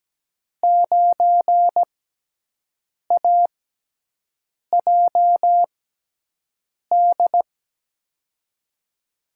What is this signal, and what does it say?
Morse code, heard as a keyed tone, about 17 wpm, spaced as for 6 wpm, 700 Hz. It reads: 9AJD
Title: USA Call Signs - 6x Character Spacing - 17wpm